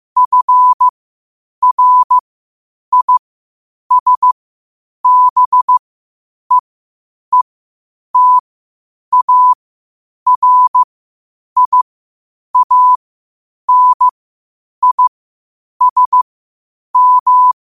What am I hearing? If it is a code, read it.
Morse code, heard as a keyed tone, about 15 wpm, spaced as for 8 wpm, 1000 Hz. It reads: FRISBEETARIANISM
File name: qotdfiles / 8wpm